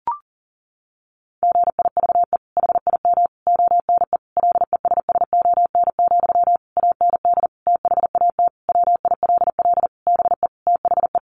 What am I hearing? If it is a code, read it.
Morse code, heard as a keyed tone, about 40 wpm, 700 Hz. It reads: GIVE HIM ONE LESSON, AND THAT WILL BE THE
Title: Sentences from Top 1200 Words 40wpm